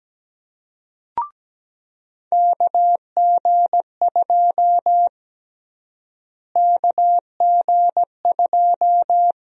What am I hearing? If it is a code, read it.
Morse code, heard as a keyed tone, about 17 wpm, 700 Hz. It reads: KG2 KG2
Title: USA Call Sign Prefixes - Repeated 2x - 17wpm